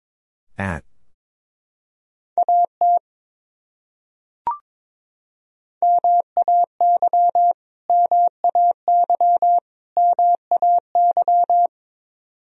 Morse code, heard as a keyed tone, about 22 wpm, 700 Hz. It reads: AT MAY MAY MAY
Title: Top 100 Words Repeated 22wpm